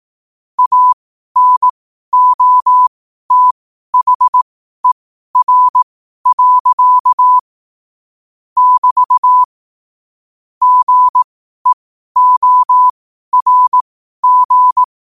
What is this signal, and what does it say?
Morse code, heard as a keyed tone, about 18 wpm, spaced as for 12 wpm, 1000 Hz. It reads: ANOTHER. <BT> GEORG
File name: qotdfiles / 12wpm